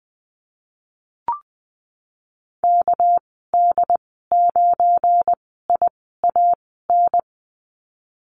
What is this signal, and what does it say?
Morse code, heard as a keyed tone, about 19 wpm, spaced as for 14 wpm, 700 Hz. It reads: KD9IAN
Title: 219 Lesson 35g - Number 0 - Call Signs 20wpm